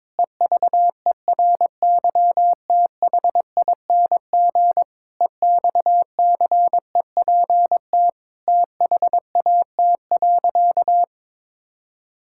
Morse code, heard as a keyed tone, about 22 wpm, 700 Hz. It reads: EVERYTHING EXCEPT THAT.
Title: Sentences from Top 1600 Words 22wpm